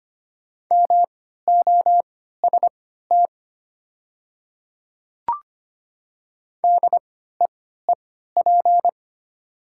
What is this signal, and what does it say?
Morse code, heard as a keyed tone, about 25 wpm, spaced as for 14 wpm, 700 Hz. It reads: MOST DEEP
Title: 3x Character Spacing - Top 500 Words - 25wpm